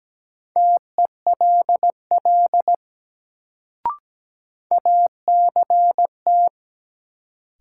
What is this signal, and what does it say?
Morse code, heard as a keyed tone, about 17 wpm, 700 Hz. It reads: TELL ACT
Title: Sentences from Top 200 Words - New Word Drill - 17wpm